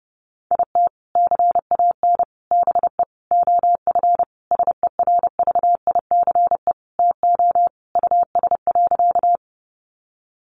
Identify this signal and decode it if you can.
Morse code, heard as a keyed tone, about 28 wpm, 700 Hz. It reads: IT CAN BE OF SERVICE TO US.
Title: Sentences from Top 1200 Words 30wpm